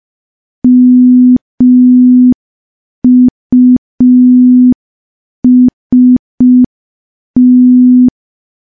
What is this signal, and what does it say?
Morse code, heard as a keyed tone, about 5 wpm, 250 Hz. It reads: MUST
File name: morse / morse-training-05WPM